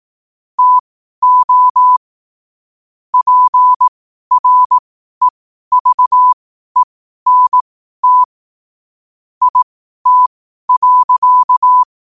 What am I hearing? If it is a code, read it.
Morse code, heard as a keyed tone, about 18 wpm, spaced as for 12 wpm, 1000 Hz. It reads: TO PREVENT IT.
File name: qotdfiles / 12wpm